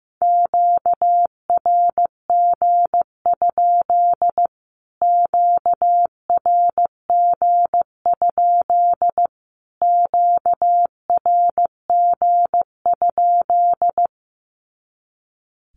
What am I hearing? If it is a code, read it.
Morse code, heard as a keyed tone, about 15 wpm, 700 Hz. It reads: QRG? QRG? QRG?
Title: Q-Codes Repeated 3x 15wpm